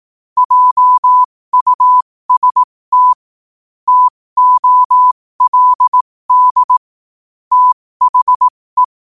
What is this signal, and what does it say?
Morse code, heard as a keyed tone, about 18 wpm, spaced as for 15 wpm, 1000 Hz. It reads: JUST TOLD THE
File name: qotdfiles / 15wpm